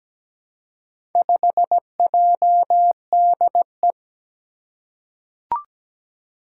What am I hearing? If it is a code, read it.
Morse code, heard as a keyed tone, about 17 wpm, 700 Hz. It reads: 5JDE
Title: Four Challenging Letter-Number Combinations 17wpm